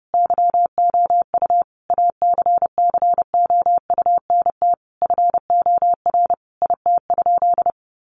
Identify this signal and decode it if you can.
Morse code, heard as a keyed tone, about 30 wpm, 700 Hz. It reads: YOU ACCOUNT FOR IT?
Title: Sentences from Top 900 Words 30wpm